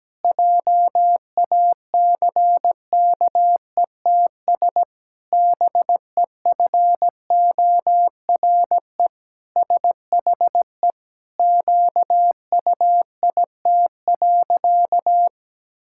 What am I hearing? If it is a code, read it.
Morse code, heard as a keyed tone, about 17 wpm, 700 Hz. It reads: JACKETS BEFORE SHE QUIT.
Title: Pangram Challenge v2 - 17wpm